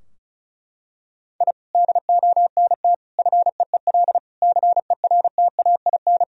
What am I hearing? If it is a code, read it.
Morse code, heard as a keyed tone, about 34 wpm, 700 Hz. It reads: I DONT FEEL CERTAIN
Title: Sentences from Top 1000 Words - Review Part 2 - 35wpm